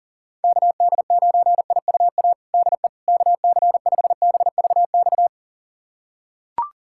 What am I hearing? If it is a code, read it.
Morse code, heard as a keyed tone, about 40 wpm, 700 Hz. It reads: KD9IUA DE KC5BVX
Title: USA Callsign DE Callsign 40wpm